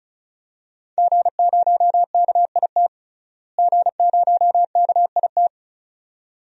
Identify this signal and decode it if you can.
Morse code, heard as a keyed tone, about 34 wpm, 700 Hz. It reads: G0KIT G0KIT
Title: DXCC-100 Call Signs - 5 Characters - Repeated 2x - 35wpm